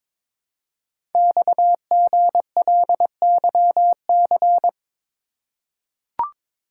Morse code, heard as a keyed tone, about 23 wpm, 700 Hz. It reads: XGLYC
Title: Code Groups - 5 Letters - 22wpm